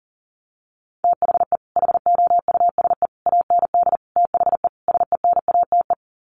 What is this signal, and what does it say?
Morse code, heard as a keyed tone, about 40 wpm, 700 Hz. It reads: THE HOUSE AND THE SENATE
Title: Sets of 5 Words Encore2 40wpm